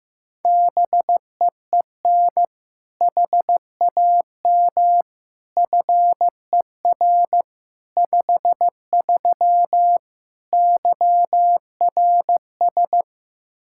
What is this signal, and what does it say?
Morse code, heard as a keyed tone, about 15 wpm, 700 Hz. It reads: BEEN HAM FER 53 YRS
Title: Practice QSO - Exchange 3 - More Info - 15wpm